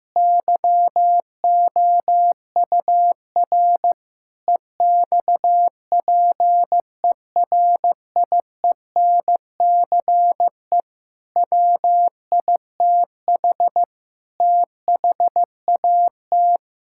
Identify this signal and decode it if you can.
Morse code, heard as a keyed tone, about 15 wpm, 700 Hz. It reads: YOUR EXPERIENCE WITH THAT
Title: Sentences from Top 1000 Words - Review Part 5 - 15wpm